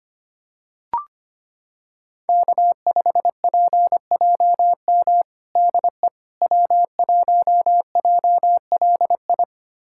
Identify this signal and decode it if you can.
Morse code, heard as a keyed tone, about 25 wpm, 700 Hz. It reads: K5PJM DE W1JLI
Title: USA Callsign DE Callsign - 5 Characters - 25wpm